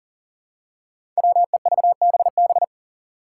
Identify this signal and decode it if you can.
Morse code, heard as a keyed tone, about 40 wpm, 700 Hz. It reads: WEVBB
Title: Five Challenging Letter-Number Combinations 40wpm